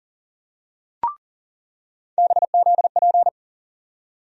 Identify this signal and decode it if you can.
Morse code, heard as a keyed tone, about 40 wpm, 700 Hz. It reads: BZP
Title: USA Call Sign Postfixes - 3 Letters 40wpm